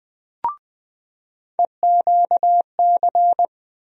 Morse code, heard as a keyed tone, about 20 wpm, 700 Hz. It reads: EQC
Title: USA Call Sign Postfixes - 3 Letters 20wpm